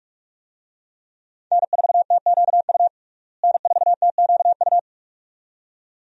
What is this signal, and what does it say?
Morse code, heard as a keyed tone, about 46 wpm, 700 Hz. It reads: N4TQU N4TQU
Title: USA Call Signs - 5 Characters - Repeated 2x - 45wpm